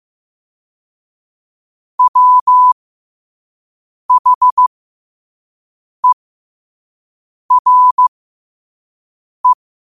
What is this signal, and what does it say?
Morse code, heard as a keyed tone, about 15 wpm, spaced as for 5 wpm, 1000 Hz. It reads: WHERE
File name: qotdfiles / 5wpm